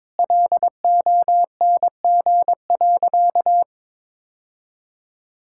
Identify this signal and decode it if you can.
Morse code, heard as a keyed tone, about 21 wpm, 700 Hz. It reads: LONG.
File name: practice / 22wpm / Sentences with 2-Word Phrases - 22wpm